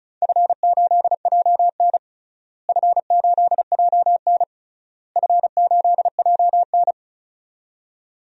Morse code, heard as a keyed tone, about 36 wpm, 700 Hz. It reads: F8JN F8JN F8JN
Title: DXCC-100 Call Signs - 4 Characters - Repeated 3x - 35wpm